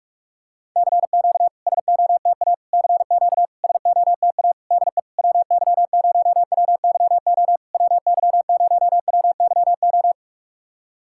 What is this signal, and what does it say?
Morse code, heard as a keyed tone, about 44 wpm, 690 Hz. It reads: CQ SOTA CQ SOTA DE WY0WYO WY0WYO
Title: SOTA QSO Practice 45wpm